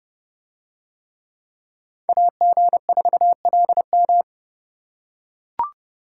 Morse code, heard as a keyed tone, about 29 wpm, 700 Hz. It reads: AG4LM